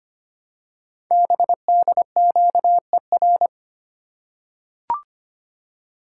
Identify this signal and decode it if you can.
Morse code, heard as a keyed tone, about 25 wpm, 700 Hz. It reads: BDQER
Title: Code Groups - 5 Letters - 25wpm